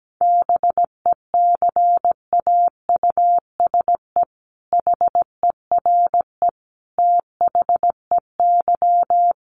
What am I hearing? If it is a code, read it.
Morse code, heard as a keyed tone, about 17 wpm, 700 Hz. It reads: BECAUSE HERE THEY
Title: Sentences from Top 1500 Words 17wpm